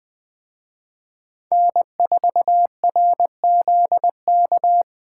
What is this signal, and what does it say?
Morse code, heard as a keyed tone, about 21 wpm, 700 Hz. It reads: N4RZK